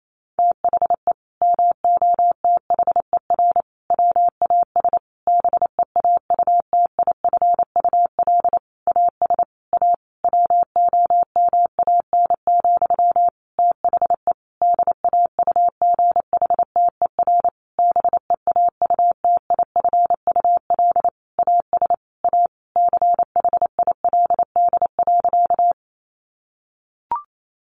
Morse code, heard as a keyed tone, about 28 wpm, 700 Hz. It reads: THE MOTHER WAS BEAUTIFUL AS A WOMAN, THE DAUGHTER BEAUTIFUL AS A CHILD.
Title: Sentences from Top 1500 Words 28wpm